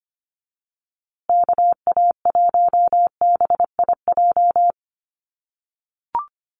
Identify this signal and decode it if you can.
Morse code, heard as a keyed tone, about 25 wpm, 700 Hz. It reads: KA1BIJ
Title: USA Call Signs Repeated 2x 25wpm